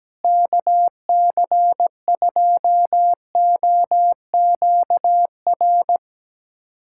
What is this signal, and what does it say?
Morse code, heard as a keyed tone, about 17 wpm, 700 Hz. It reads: KC2OQR
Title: USA Callsign DE Callsign 17wpm